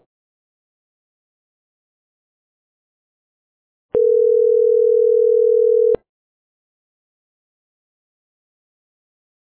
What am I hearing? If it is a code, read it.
Ringback tone (North America), 440 + 480 Hz.